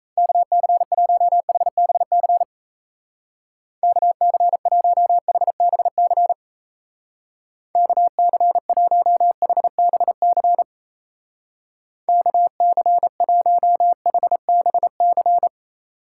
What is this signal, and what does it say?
Morse code, heard as a keyed tone, about 33 wpm, 700 Hz. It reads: KC1HBC KC1HBC KC1HBC KC1HBC
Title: Speed Racing - USA 2x3 Call Signs 28wpm